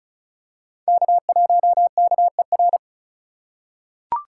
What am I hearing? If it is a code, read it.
Morse code, heard as a keyed tone, about 36 wpm, 700 Hz. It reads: K1KER
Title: USA Call Signs - 2x Character Spacing - 35wpm